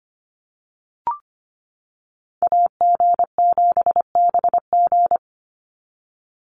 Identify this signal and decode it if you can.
Morse code, heard as a keyed tone, about 25 wpm, 700 Hz. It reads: AG7BG
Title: USA Call Signs 25wpm